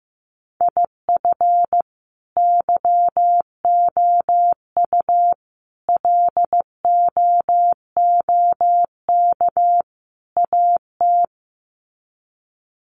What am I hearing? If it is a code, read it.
Morse code, heard as a keyed tone, about 15 wpm, 700 Hz. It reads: IF YOU LOOK AT
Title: Sets of 4 Words 15wpm